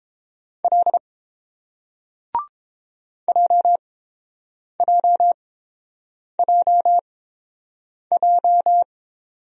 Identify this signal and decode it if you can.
Morse code, heard as a keyed tone, about 30 wpm, spaced as for 7 wpm, 700 Hz. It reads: L JJJJ